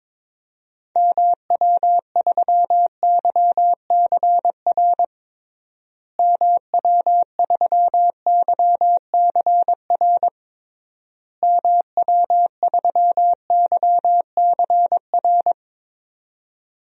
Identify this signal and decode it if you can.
Morse code, heard as a keyed tone, about 22 wpm, 700 Hz. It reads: MW3YCR MW3YCR MW3YCR
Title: DXCC-100 Call Signs - 6 Characters - Repeated 3x - 22wpm